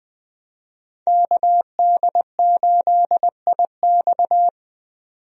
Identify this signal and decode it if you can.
Morse code, heard as a keyed tone, about 20 wpm, 700 Hz. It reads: KD8IX
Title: USA Call Signs - 5 Characters - 20wpm